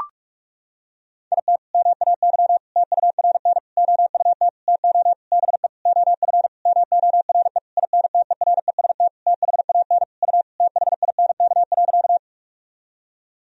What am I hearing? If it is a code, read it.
Morse code, heard as a keyed tone, about 42 wpm, 700 Hz. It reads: IT MAY TURN OUT TO BE OF MORE INTEREST THAN U THINK.
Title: Sentences from Top 400 Words and Abbreviations 45wpm